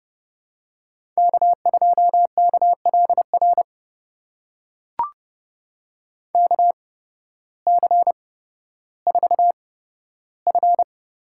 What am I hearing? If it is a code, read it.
Morse code, heard as a keyed tone, about 32 wpm, spaced as for 14 wpm, 700 Hz. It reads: K2KLR KC4F
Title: USA Call Signs - 5 Characters - 8x Character Spacing - 30wpm